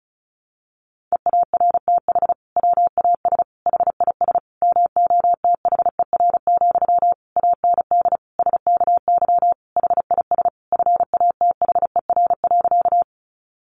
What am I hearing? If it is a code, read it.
Morse code, heard as a keyed tone, about 34 wpm, 700 Hz. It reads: EARTH WAS HIS MOTHER, AND SKY HIS FATHER.
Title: Sentences from Top 1200 Words 35wpm